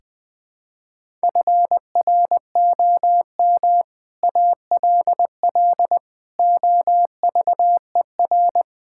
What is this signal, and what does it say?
Morse code, heard as a keyed tone, about 20 wpm, 700 Hz. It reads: FROM ALL OVER
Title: Build A Phrase - 5-Word Phrases - 20wpm